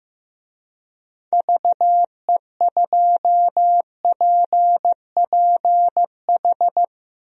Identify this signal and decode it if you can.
Morse code, heard as a keyed tone, about 15 wpm, 700 Hz. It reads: VE2PPH
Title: DXCC-100 Call Signs - 6 Characters - Repeated 3x - 15wpm